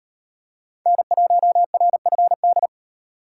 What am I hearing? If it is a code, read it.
Morse code, heard as a keyed tone, about 38 wpm, 700 Hz. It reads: N1RFD